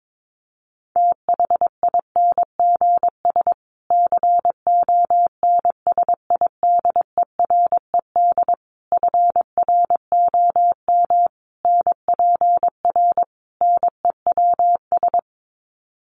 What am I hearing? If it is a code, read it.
Morse code, heard as a keyed tone, about 22 wpm, 700 Hz. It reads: THINGS CONSIDERED FROM NPR NEWS
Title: Sets of 5 Words Encore 22wpm